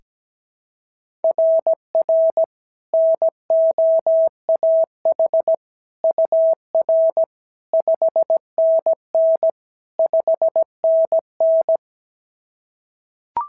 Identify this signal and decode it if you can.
Morse code, heard as a keyed tone, about 16 wpm, 650 Hz. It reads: RR NOAH UR 5NN 5NN
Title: SOTA QSO Practice 17wpm